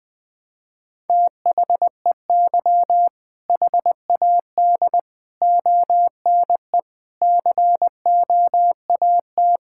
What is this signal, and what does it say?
Morse code, heard as a keyed tone, about 20 wpm, 700 Hz. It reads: THEY HAD ONE COAT
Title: Sentences from Top 1600 Words 20wpm